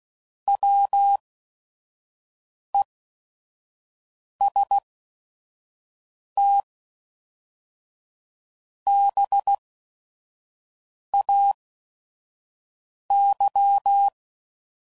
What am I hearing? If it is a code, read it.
Morse code, heard as a keyed tone, about 16 wpm, spaced as for 5 wpm, 800 Hz. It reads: WESTBAY
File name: CW Code Groups @5WPM / fromSM's / Cities